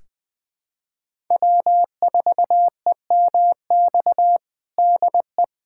Morse code, heard as a keyed tone, about 20 wpm, 700 Hz. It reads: W4EMX DE
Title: USA Callsign DE Callsign - 5 Characters - 20wpm